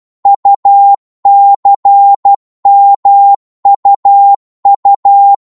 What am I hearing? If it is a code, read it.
Morse code, heard as a keyed tone, about 12 wpm, 800 Hz. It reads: UCMUU
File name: Morse / ltr-12wpm